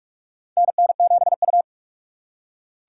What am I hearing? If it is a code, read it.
Morse code, heard as a keyed tone, about 44 wpm, 700 Hz. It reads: NN7U
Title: USA Call Signs - 4 Characters - 6x Character Spacing - 45wpm